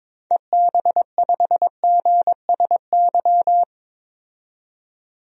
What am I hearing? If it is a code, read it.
Morse code, heard as a keyed tone, about 22 wpm, 700 Hz. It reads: EB5GSY
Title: DXCC-100 Call Signs - 6 Characters - 22wpm